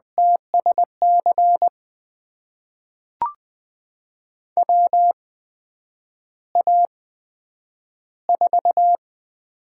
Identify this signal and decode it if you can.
Morse code, heard as a keyed tone, about 20 wpm, spaced as for 9 wpm, 700 Hz. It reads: TSC WA4
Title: USA Call Signs - 6 Characters - 8x Character Spacing - 20wpm